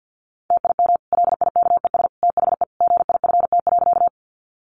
Morse code, heard as a keyed tone, about 50 wpm, 700 Hz. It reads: TIM LIKES THE GIFT.
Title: Easy Phrases 50wpm